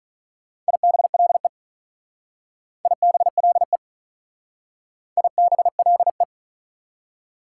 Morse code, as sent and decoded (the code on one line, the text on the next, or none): .. -... .-.. / .. -... .-.. / .. -... .-.. .
IBLE IBLE IBLE